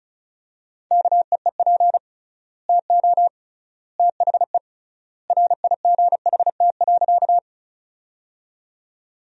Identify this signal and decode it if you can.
Morse code, heard as a keyed tone, about 36 wpm, 700 Hz. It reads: KEEP TO THE RIGHT.
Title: 3x Word-Spacing - Sentences from Top 300 Words - 35wpm